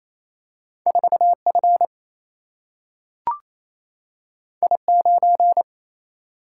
Morse code, heard as a keyed tone, about 28 wpm, 700 Hz. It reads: 4F I9
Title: Single-Letter Single-Number 28wpm